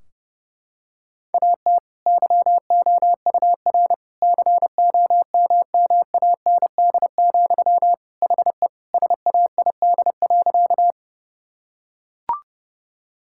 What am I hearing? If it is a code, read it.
Morse code, heard as a keyed tone, about 32 wpm, 700 Hz. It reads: AT YOUR COMMAND, HE SAID.
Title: Sentences from Top 1600 Words 30wpm